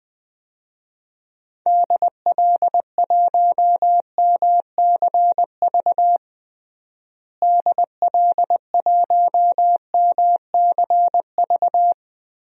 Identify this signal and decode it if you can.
Morse code, heard as a keyed tone, about 20 wpm, 700 Hz. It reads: DL1MCV DL1MCV
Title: DXCC-100 Call Signs - 6 Characters - Repeated 3x - 20wpm